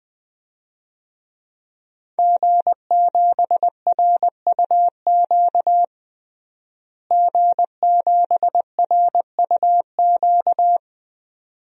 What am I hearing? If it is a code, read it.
Morse code, heard as a keyed tone, about 20 wpm, 700 Hz. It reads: G7RUQ G7RUQ
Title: DXCC-100 Call Signs - 5 Characters - Repeated 2x - 20wpm